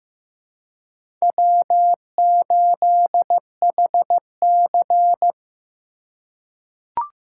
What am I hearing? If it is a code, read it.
Morse code, heard as a keyed tone, about 15 wpm, 700 Hz. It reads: W8HC
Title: USA Call Signs - 4 Characters - 3x Character Spacing - 15wpm